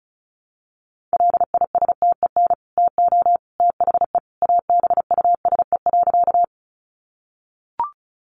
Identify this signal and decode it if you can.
Morse code, heard as a keyed tone, about 34 wpm, 700 Hz. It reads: LISTEN TO THE ABUSE.
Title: Sentences from Top 1300 Words 35wpm